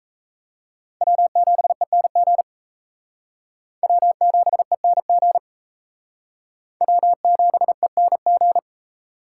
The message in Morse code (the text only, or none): W7ENG W7ENG W7ENG